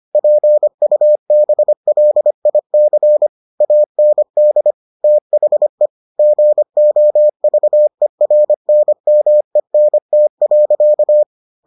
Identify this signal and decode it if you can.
Morse code, heard as a keyed tone, about 25 wpm, 600 Hz. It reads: PUBLIC AND THE GOVERNMENT.